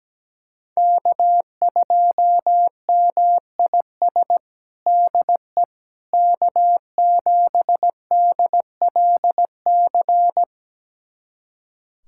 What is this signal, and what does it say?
Morse code, heard as a keyed tone, about 17 wpm, 700 Hz. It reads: K2MIS DE K7DLC
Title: USA Callsign DE Callsign - 5 Characters - 17wpm